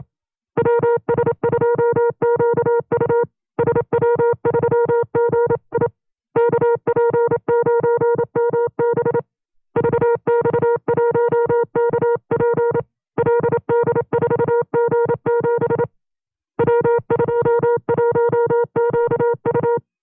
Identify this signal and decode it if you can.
Morse code, heard as a keyed tone, about 27 wpm, 470 Hz. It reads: WS2QU SW3GI KP9MB VX1KP LD4G7 W21QU
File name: FreeFiles / CW-practice / 25WPM-side1